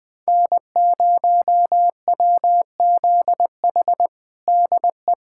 Morse code, heard as a keyed tone, about 20 wpm, 700 Hz. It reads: N0WZH DE
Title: USA Callsign DE Callsign - 5 Characters - 20wpm